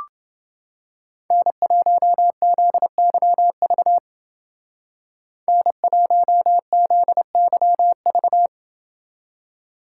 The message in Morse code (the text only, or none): N1ZYV N1ZYV